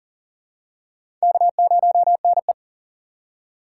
Morse code, heard as a keyed tone, about 40 wpm, 700 Hz. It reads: K0NE
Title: USA Call Signs - 2x Character Spacing - 40wpm